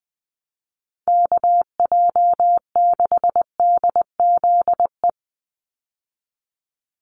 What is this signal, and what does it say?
Morse code, heard as a keyed tone, about 20 wpm, 700 Hz. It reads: KJ6DZE
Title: USA Call Signs 20wpm